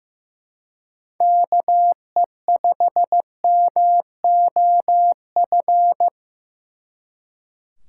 Morse code, heard as a keyed tone, about 17 wpm, 700 Hz. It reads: KE5MOF
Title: Speed Racing - USA Call Signs 15wpm